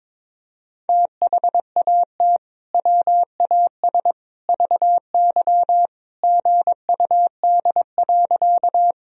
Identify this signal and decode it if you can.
Morse code, heard as a keyed tone, about 21 wpm, 700 Hz. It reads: THAT WAS VY GUD.
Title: Sentences from Top 200 Words and Abbreviations 22wpm